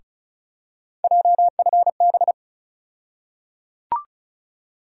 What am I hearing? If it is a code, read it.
Morse code, heard as a keyed tone, about 36 wpm, 700 Hz. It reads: JFB
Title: USA Call Sign Postfixes - 3 Letters 35wpm